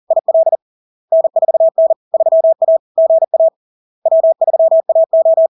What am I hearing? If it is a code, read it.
Morse code, heard as a keyed tone, about 40 wpm, 650 Hz. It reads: IL N4N 3A GA W3AO
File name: LARK-FD-2022-Log-in-CW-40wpm-650-Hz